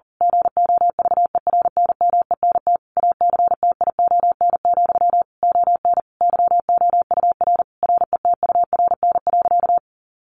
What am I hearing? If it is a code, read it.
Morse code, heard as a keyed tone, about 40 wpm, 700 Hz. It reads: GOVERNMENT ACTION, ON YOUR RETURN.
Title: A Sound Of Thunder 40wpm